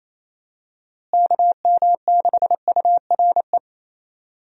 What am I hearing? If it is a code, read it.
Morse code, heard as a keyed tone, about 28 wpm, 700 Hz. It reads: KM6URE
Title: USA Call Signs - 6 Characters - Repeated 3x - 28wpm